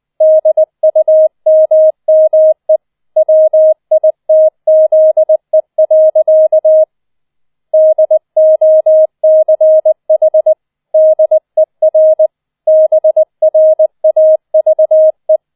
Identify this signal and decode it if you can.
Morse code, heard as a keyed tone, about 20 wpm, 630 Hz. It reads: DUMME WITZE. DOCH DER BRAVE